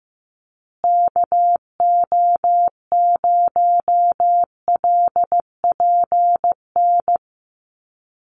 Morse code, heard as a keyed tone, about 15 wpm, 700 Hz. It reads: KO0LPN
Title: USA Call Signs Repeated 3x 15wpm